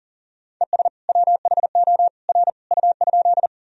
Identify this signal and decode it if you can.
Morse code, heard as a keyed tone, about 38 wpm, 700 Hz. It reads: ES WHO R U?
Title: Sentences from Top 100 Words and Abbreviations 40wpm